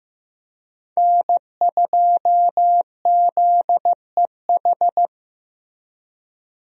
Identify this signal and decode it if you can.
Morse code, heard as a keyed tone, about 15 wpm, 700 Hz. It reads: N2ZEH
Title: USA Call Signs - 5 Characters - Repeated 2x - 15wpm